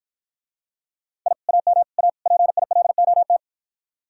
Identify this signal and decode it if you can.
Morse code, heard as a keyed tone, about 51 wpm, 700 Hz. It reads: I AM A PILOT